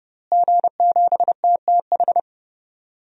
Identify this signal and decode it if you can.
Morse code, heard as a keyed tone, about 32 wpm, 700 Hz. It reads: G7TTH
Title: DXCC-100 Call Signs - 5 Characters - Repeated 3x - 30wpm